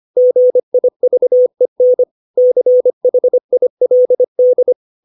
Morse code, heard as a keyed tone, about 25 wpm, 500 Hz. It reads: GIVEN CHILD
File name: 632 words / 632 Words 25WPM